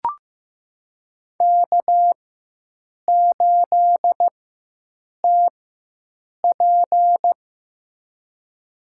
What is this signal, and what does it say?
Morse code, heard as a keyed tone, about 15 wpm, spaced as for 7 wpm, 700 Hz. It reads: K8TP
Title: USA Call Signs - 4x Character Spacing - 15wpm